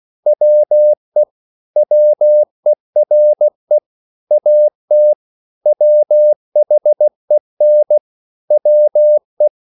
Morse code, heard as a keyed tone, about 16 wpm, 600 Hz. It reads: WE WERE AT WHEN WE
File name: CW / Practice / EnglishQuotations0216